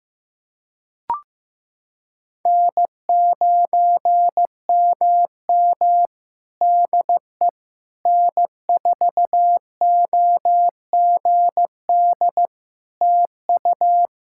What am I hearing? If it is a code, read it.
Morse code, heard as a keyed tone, about 15 wpm, 700 Hz. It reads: N9MM DE N4OGD TU
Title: K1USN SST QSO Practice 15wpm